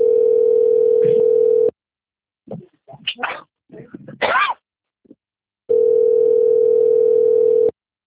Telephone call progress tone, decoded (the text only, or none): ringback tone